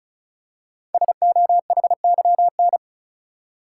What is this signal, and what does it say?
Morse code, heard as a keyed tone, about 36 wpm, 700 Hz. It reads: SOHYN
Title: Code Groups - 5 Letters - 35wpm